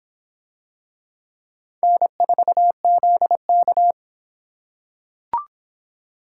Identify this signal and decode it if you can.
Morse code, heard as a keyed tone, about 26 wpm, 700 Hz. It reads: N4ZK